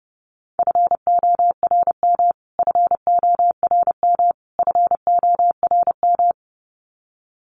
Morse code, heard as a keyed tone, about 30 wpm, 700 Hz. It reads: FORM FORM FORM
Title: Top 500 Words Repeated 30wpm